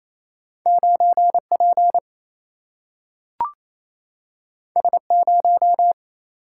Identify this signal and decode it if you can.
Morse code, heard as a keyed tone, about 28 wpm, 700 Hz. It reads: 9 P S 0
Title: Single-Letter Single-Number 28wpm